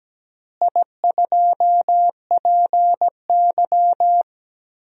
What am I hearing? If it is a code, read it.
Morse code, heard as a keyed tone, about 17 wpm, 700 Hz. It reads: I2PY
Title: DXCC-100 Call Signs - 4 Characters - Repeated 3x - 17wpm